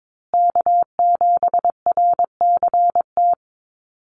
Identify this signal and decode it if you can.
Morse code, heard as a keyed tone, about 22 wpm, 700 Hz. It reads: K7RCT
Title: USA Call Signs Repeated 3x 22wpm